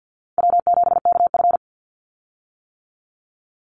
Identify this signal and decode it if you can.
Morse code, heard as a keyed tone, about 50 wpm, 700 Hz. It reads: W7KF
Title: Easy Call Signs - 50wpm